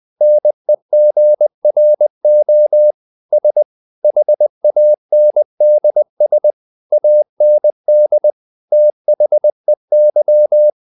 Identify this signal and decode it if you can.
Morse code, heard as a keyed tone, about 20 wpm, 600 Hz. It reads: NEGRO S HANDS AND THEY
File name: the old man and the sea part 2 20wpm